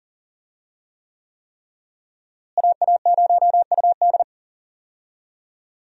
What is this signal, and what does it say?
Morse code, heard as a keyed tone, about 40 wpm, 700 Hz. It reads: AA0UD
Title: USA Call Signs - 5 Characters - Repeated 3x - 40wpm